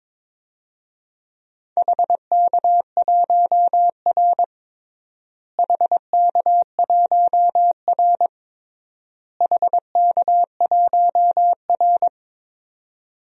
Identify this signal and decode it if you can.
Morse code, heard as a keyed tone, about 22 wpm, 700 Hz. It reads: HK1R HK1R HK1R
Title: DXCC-100 Call Signs - 4 Characters - Repeated 3x - 22wpm